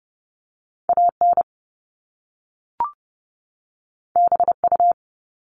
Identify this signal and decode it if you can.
Morse code, heard as a keyed tone, about 30 wpm, 700 Hz. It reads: AN BU
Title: Top Two Letter Combinations in Words 30wpm